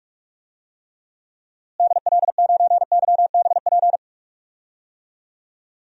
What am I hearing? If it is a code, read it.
Morse code, heard as a keyed tone, about 46 wpm, 700 Hz. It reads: DL9YBP
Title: DXCC-100 Call Signs - 6 Characters - Repeated 2x - 45wpm